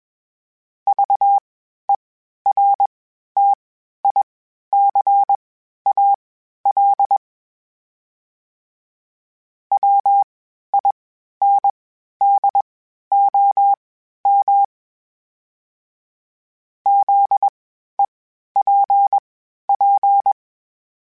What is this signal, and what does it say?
Morse code, heard as a keyed tone, about 21 wpm, spaced as for 12 wpm, 800 Hz. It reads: VERTICAL WINDOM ZEPP
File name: (spaced) / FromSMs spaced mp3s / Antennas sp@25